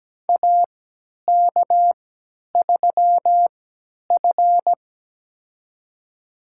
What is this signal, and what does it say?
Morse code, heard as a keyed tone, about 17 wpm, spaced as for 10 wpm, 700 Hz. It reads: AK3F
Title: USA Call Signs - 4 Characters - 3x Character Spacing - 17wpm